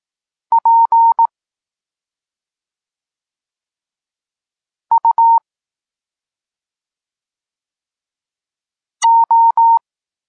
Morse code, heard as a keyed tone, about 18 wpm, spaced as for 2 wpm, 920 Hz. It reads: PUO
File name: Tekst-34 4